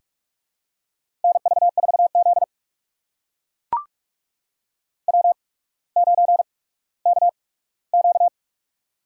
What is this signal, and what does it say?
Morse code, heard as a keyed tone, about 44 wpm, spaced as for 19 wpm, 700 Hz. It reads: NV4Z W9KQ
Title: USA Call Signs - 4 Characters - 8x Character Spacing - 45wpm